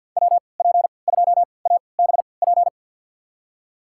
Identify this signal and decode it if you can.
Morse code, heard as a keyed tone, about 48 wpm, spaced as for 28 wpm, 700 Hz. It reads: WP2ABP
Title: USA Call Signs - 6 Characters - 3x Character Spacing - 50wpm